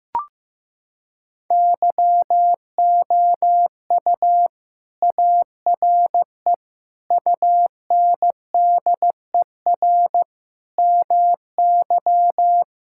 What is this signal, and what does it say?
Morse code, heard as a keyed tone, about 15 wpm, 700 Hz. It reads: YOU ARE UNDER MY
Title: Sentences from Top 1600 Words 15wpm